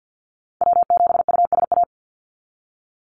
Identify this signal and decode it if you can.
Morse code, heard as a keyed tone, about 48 wpm, 700 Hz. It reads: W7USA
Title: NA SOTA Chaser Callsigns 50wpm